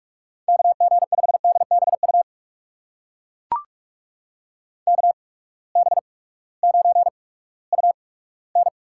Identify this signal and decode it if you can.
Morse code, heard as a keyed tone, about 42 wpm, spaced as for 20 wpm, 700 Hz. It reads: KG5DBU KB9UN
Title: USA Call Signs - 6 Characters - 8x Character Spacing - 45wpm